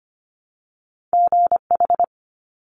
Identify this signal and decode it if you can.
Morse code, heard as a keyed tone, about 25 wpm, 700 Hz. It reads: GH
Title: Top Two Letter Combinations in Words 25wpm